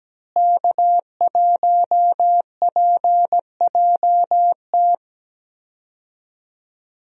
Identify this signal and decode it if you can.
Morse code, heard as a keyed tone, about 17 wpm, 700 Hz. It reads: K1PJT
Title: USA Call Signs - 5 Characters - Repeated 3x - 17wpm